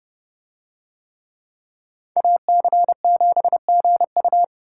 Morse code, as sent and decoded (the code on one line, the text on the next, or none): .- -.-. --... --. ..-
AC7GU